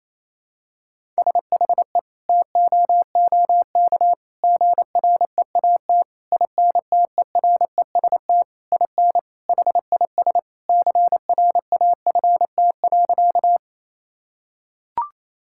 Morse code, heard as a keyed tone, about 28 wpm, 700 Hz. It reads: SHE TOOK GREAT INTEREST IN HIS CRAFT.
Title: Sentences from Top 1000 Words - Review Part 6 - 28wpm